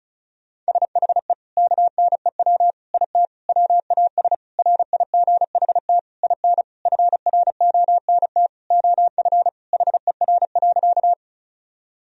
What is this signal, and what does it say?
Morse code, heard as a keyed tone, about 36 wpm, 700 Hz. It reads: SHE KNEW IT WAS RIGHT IN FRONT OF HER.
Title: Sentences from Top 1000 Words - Review Part 2 - 35wpm